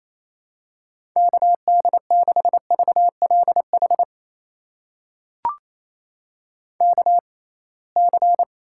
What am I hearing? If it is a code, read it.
Morse code, heard as a keyed tone, about 28 wpm, spaced as for 21 wpm, 700 Hz. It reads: KD6VLH KC
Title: USA Call Signs - 6 Characters - 6x Character Spacing - 28wpm